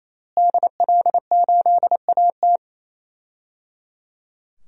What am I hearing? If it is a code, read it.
Morse code, heard as a keyed tone, about 28 wpm, 700 Hz. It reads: DL8AT
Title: DXCC-100 Call Signs - 5 Characters - Repeated 3x - 28wpm